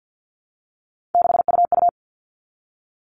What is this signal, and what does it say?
Morse code, heard as a keyed tone, about 53 wpm, 700 Hz. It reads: 6UU
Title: All Cut Numbers - 3-Digit 50wpm